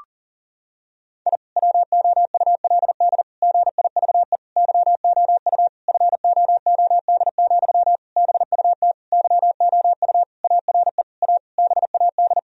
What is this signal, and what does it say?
Morse code, heard as a keyed tone, about 40 wpm, 700 Hz. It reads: I WOULD GIVE YOU FOOD, BUT YOU ARE A BAD
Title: Sentences from Top 1000 Words - Review Part 1 - 40wpm